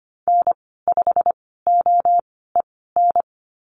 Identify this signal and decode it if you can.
Morse code, heard as a keyed tone, about 25 wpm, spaced as for 16 wpm, 690 Hz. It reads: N5OEN
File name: Lesson 15g - Number 5 - Call Signs 25wpm